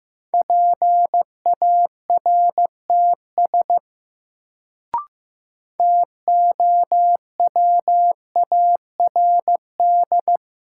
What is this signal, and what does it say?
Morse code, heard as a keyed tone, about 15 wpm, 700 Hz. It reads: PARTS TOWARD
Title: Sentences from Top 1000 Words - Review Part 6 - New Word Drill - 15wpm